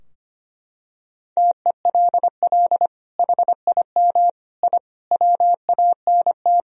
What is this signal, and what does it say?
Morse code, heard as a keyed tone, about 25 wpm, 700 Hz. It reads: TELL HIM I WANT
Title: Sentences from Top 1000 Words - Review Part 2 - 25wpm